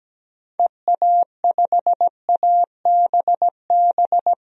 Morse code, heard as a keyed tone, about 17 wpm, 700 Hz. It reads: EA5ABB
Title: DXCC-100 Call Signs - 6 Characters - 17wpm